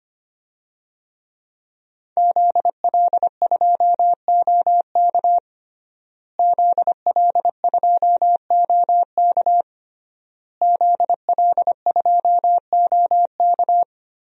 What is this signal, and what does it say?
Morse code, heard as a keyed tone, about 25 wpm, 700 Hz. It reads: ZL2OK ZL2OK ZL2OK
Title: DXCC-100 Call Signs - 5 Characters - Repeated 3x - 25wpm